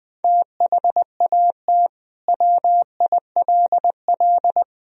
Morse code, heard as a keyed tone, about 20 wpm, 700 Hz. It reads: THAT WILL
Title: Sentences from Top 1000 Words - Review Part 3 - 20wpm